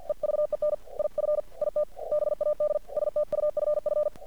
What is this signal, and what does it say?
Morse code, heard as a keyed tone, about 50 wpm, 600 Hz. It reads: EVEN IF IT BAD STUFF
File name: archived / CRSnet